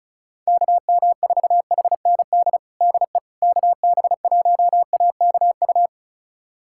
Morse code, heard as a keyed tone, about 36 wpm, 700 Hz. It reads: KM4HND DE KB1AKU
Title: USA Callsign DE Callsign 35wpm